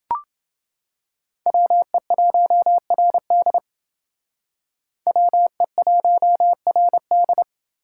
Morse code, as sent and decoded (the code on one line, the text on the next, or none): .-- . .---- .-. -.. / .-- . .---- .-. -..
WE1RD WE1RD